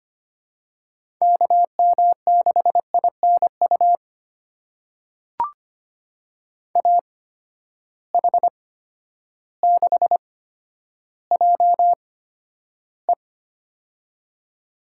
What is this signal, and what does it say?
Morse code, heard as a keyed tone, about 25 wpm, spaced as for 11 wpm, 700 Hz. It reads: KM6INU AH6JE
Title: USA Call Signs - 8x Character Spacing - 25wpm